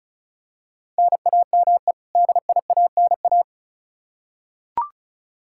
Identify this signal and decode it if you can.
Morse code, heard as a keyed tone, about 36 wpm, 700 Hz. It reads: NAME DIANA
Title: Practice QSO - Exchange 1 - Beginning - 35wpm